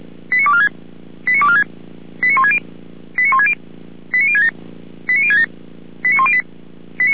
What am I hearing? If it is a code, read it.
Five-tone selective call (ZVEI), heard as quick runs of five tones, tones of about 70 ms. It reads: 89246 89246 89160 89160 89067 89067 89108